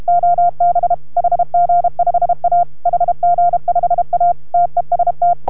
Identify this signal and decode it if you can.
Morse code, heard as a keyed tone, about 32 wpm, 690 Hz. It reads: OB HG5A HG5A TEST